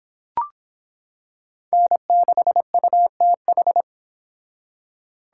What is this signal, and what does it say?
Morse code, heard as a keyed tone, about 26 wpm, 700 Hz. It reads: N6UTH